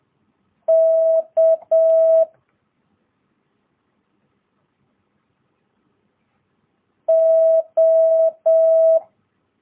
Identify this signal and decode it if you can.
Morse code, heard as a keyed tone, about 7 wpm, 650 Hz. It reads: K O
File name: jh8cbh / 2022cw